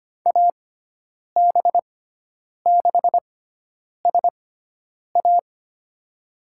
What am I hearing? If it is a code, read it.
Morse code, heard as a keyed tone, about 25 wpm, spaced as for 9 wpm, 700 Hz. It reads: AB6SA
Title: USA Call Signs - 5 Characters - 6x Character Spacing - 25wpm